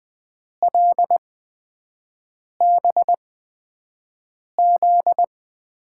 Morse code, heard as a keyed tone, about 20 wpm, spaced as for 5 wpm, 700 Hz. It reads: LBZ
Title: USA Call Signs - 6 Characters - 8x Character Spacing - 20wpm